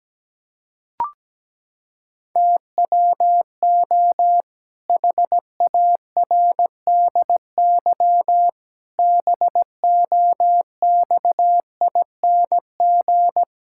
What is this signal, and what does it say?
Morse code, heard as a keyed tone, about 17 wpm, 700 Hz. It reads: TWO HARDY BOXING
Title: Pangram Challenge v2 - 17wpm